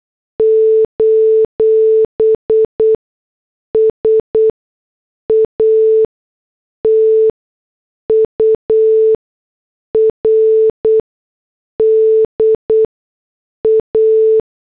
Morse code, heard as a keyed tone, about 8 wpm, spaced as for 6 wpm, 440 Hz. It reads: : SATURDA
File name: morse / headlines